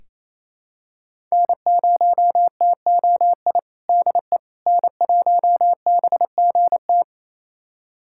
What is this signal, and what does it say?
Morse code, heard as a keyed tone, about 28 wpm, 700 Hz. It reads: N0TOI DE N1BGT
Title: USA Callsign DE Callsign - 5 Characters - 28wpm